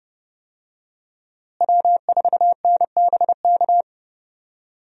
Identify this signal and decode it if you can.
Morse code, heard as a keyed tone, about 30 wpm, 700 Hz. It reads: W4NBK
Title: USA Call Signs - 5 Characters - Repeated 2x - 30wpm